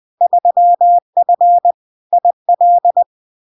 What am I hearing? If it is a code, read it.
Morse code, heard as a keyed tone, about 20 wpm, 700 Hz. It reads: 3F IL
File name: LARK-FD-2022-Log-in-CW-20wpm-700-Hz